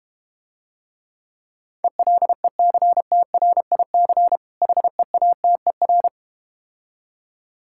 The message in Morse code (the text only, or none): ELECTRIC HEATER